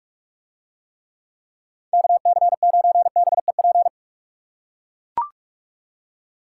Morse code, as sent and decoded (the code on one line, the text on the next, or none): -.- -.-. ----. -... .--.
KC9BEP